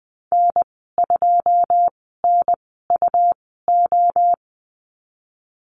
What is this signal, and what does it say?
Morse code, heard as a keyed tone, about 20 wpm, spaced as for 15 wpm, 700 Hz. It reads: N2NUO
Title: 186 Lesson 30g - Letter b - Call Signs 20wpm